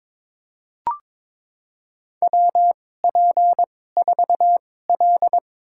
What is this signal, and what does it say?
Morse code, heard as a keyed tone, about 22 wpm, spaced as for 16 wpm, 700 Hz. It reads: WP4L
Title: USA Call Signs - 4 Characters - 2x Character Spacing - 22wpm